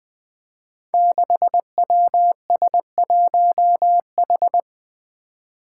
Morse code, heard as a keyed tone, about 20 wpm, 700 Hz. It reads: 6WS1H
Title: Five Challenging Letter-Number Combinations 20wpm